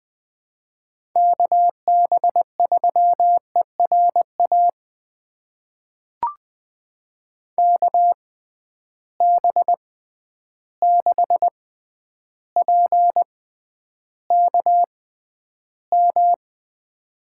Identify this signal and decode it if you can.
Morse code, heard as a keyed tone, about 20 wpm, spaced as for 10 wpm, 700 Hz. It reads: KB3ERA KB6PKM
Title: USA Call Signs - 6 Characters - 6x Character Spacing - 20wpm